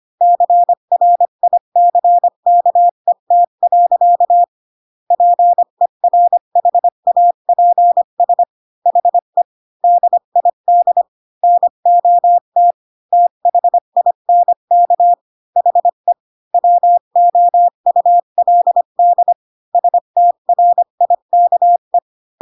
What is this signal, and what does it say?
Morse code, heard as a keyed tone, about 25 wpm, 700 Hz. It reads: CRICKET. PERHAPS HE DID NOT THINK HE WOULD STRIKE
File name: Pinocchio / Pinocchio ch4 20 25 ik1hge